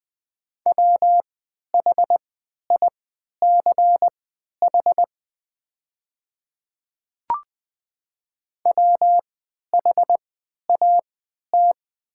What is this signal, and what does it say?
Morse code, heard as a keyed tone, about 20 wpm, spaced as for 11 wpm, 690 Hz. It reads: WHICH WHAT
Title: 3x Character Spacing - Top 100 Words - 20wpm